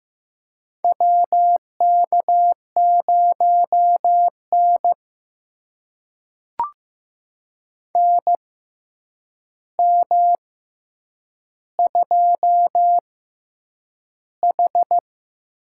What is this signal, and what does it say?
Morse code, heard as a keyed tone, about 15 wpm, spaced as for 8 wpm, 700 Hz. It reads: WK0N NM2H
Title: USA Call Signs - 4 Characters - 6x Character Spacing - 15wpm